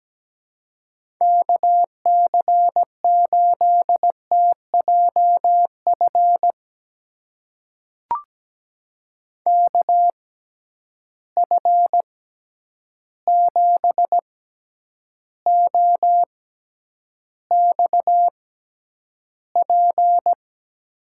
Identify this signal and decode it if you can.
Morse code, heard as a keyed tone, about 17 wpm, spaced as for 9 wpm, 700 Hz. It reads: KC8TJF KF7OXP
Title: USA Call Signs - 6 Characters - 6x Character Spacing - 17wpm